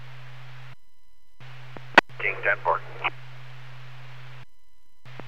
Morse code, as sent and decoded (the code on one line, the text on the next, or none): none